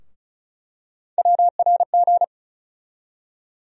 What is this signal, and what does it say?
Morse code, heard as a keyed tone, about 36 wpm, 700 Hz. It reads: WRG
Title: USA Call Sign Postfixes - 3 Letters 35wpm